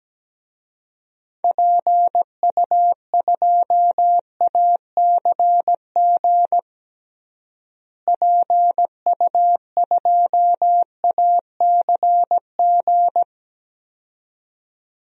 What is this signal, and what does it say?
Morse code, heard as a keyed tone, about 17 wpm, 700 Hz. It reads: PU2ACG PU2ACG
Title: DXCC-100 Call Signs - 6 Characters - Repeated 2x - 17wpm